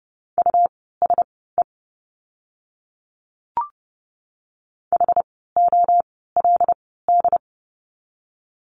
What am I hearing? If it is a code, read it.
Morse code, heard as a keyed tone, about 30 wpm, spaced as for 17 wpm, 700 Hz. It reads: USE HOLD
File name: Lesson 17f - Letter u - Words 30wpm